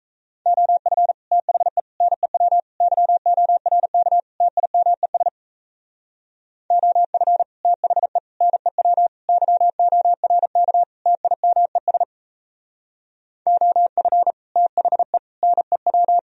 Morse code, heard as a keyed tone, about 36 wpm, 700 Hz. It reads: OF THE NEW YORK TIMES OF THE NEW YORK TIMES OF THE NEW